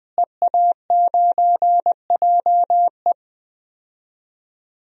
Morse code, heard as a keyed tone, about 20 wpm, 700 Hz. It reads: EA9JE
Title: DXCC-100 Call Signs - 5 Characters - Repeated 2x - 20wpm